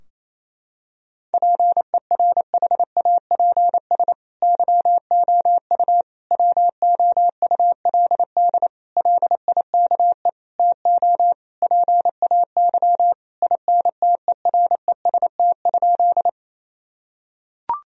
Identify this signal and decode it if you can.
Morse code, heard as a keyed tone, about 27 wpm, 700 Hz. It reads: PERHAPS YOU WOULD LIKE TO PAY INTEREST?
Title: Sentences from Top 1000 Words - Review Part 2 - 28wpm